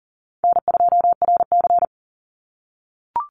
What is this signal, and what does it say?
Morse code, heard as a keyed tone, about 40 wpm, 700 Hz. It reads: N2RC
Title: Easy Call Signs - 40wpm